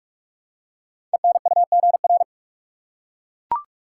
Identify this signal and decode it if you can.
Morse code, heard as a keyed tone, about 44 wpm, 700 Hz. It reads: ENUGR
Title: Code Groups - 5 Letters - 45wpm